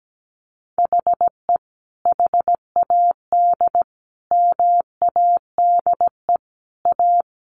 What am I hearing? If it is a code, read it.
Morse code, heard as a keyed tone, about 17 wpm, 700 Hz. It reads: HE HAD MADE A
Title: Sentences from Top 900 Words 17wpm